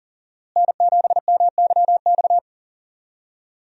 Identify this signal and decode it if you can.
Morse code, heard as a keyed tone, about 40 wpm, 700 Hz. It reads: N7MYX
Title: USA Call Signs - 5 Characters - Repeated 2x - 40wpm